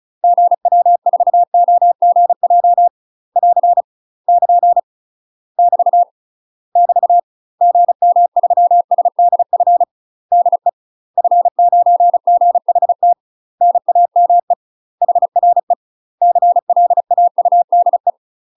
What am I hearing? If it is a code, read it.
Morse code, heard as a keyed tone, about 35 wpm, 700 Hz. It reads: GW4OGJ <AR> <KN> <BT> <BT> GM3SDF DE F9GHT NAME HRE CLAUDE
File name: UK MASTER QSOs TRAINING / 35WPM / UK MASTER QSOs 0018